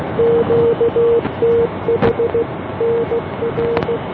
Morse code, heard as a keyed tone, about 16 wpm, 470 Hz, with noise. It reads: QTH NR